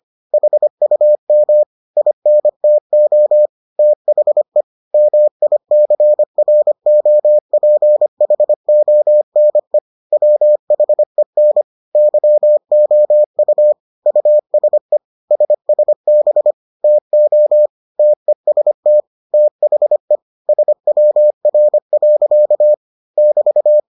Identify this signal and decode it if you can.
Morse code, heard as a keyed tone, about 25 wpm, 600 Hz. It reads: HUM INTO THE MICROPHONE WHEN YOU USE SSB TO TEST THE SWR. <BT>